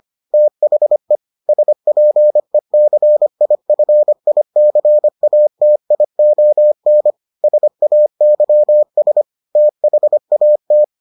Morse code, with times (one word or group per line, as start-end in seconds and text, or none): 0.3-1.2 THE
1.5-7.1 SPECIFICATION
7.4-9.2 SAYS
9.6-10.8 THAT